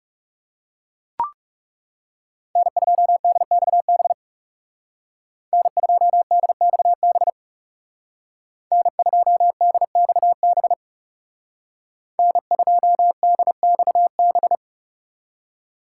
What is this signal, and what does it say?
Morse code, heard as a keyed tone, about 39 wpm, 700 Hz. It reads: N2DXB N2DXB N2DXB N2DXB